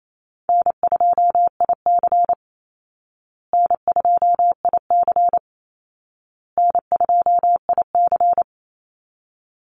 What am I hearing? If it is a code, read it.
Morse code, heard as a keyed tone, about 28 wpm, 700 Hz. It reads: N2IC N2IC N2IC
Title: Easy Call Signs Repeated 3x - 28wpm